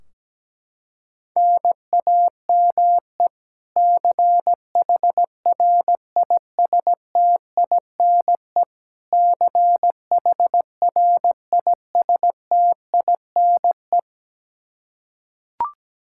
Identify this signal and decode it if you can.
Morse code, heard as a keyed tone, about 17 wpm, 700 Hz. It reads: NAME CHRISTINE CHRISTINE
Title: Practice QSO - Exchange 1 - Beginning - 17wpm